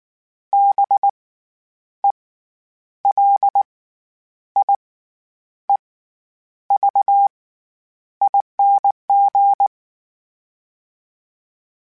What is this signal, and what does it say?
Morse code, heard as a keyed tone, about 19 wpm, spaced as for 9 wpm, 800 Hz. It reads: BELIEVING
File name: Special INGs / Special INGs-mp3s / SlowToFaster ing @18